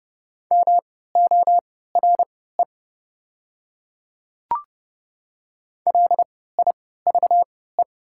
Morse code, heard as a keyed tone, about 31 wpm, spaced as for 17 wpm, 700 Hz. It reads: MORE LIVE
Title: 3x Character Spacing - Top 200 Words - 30wpm